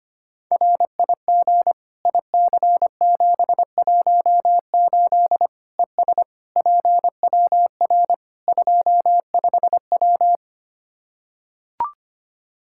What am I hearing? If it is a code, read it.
Morse code, heard as a keyed tone, about 25 wpm, 700 Hz. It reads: RIG IC718 ES PWR 25W
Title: K1USN SST QSO Practice 25wpm